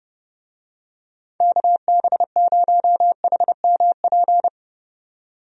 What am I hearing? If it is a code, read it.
Morse code, heard as a keyed tone, about 32 wpm, 700 Hz. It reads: KB0HMP
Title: USA Call Signs - 6 Characters - 30wpm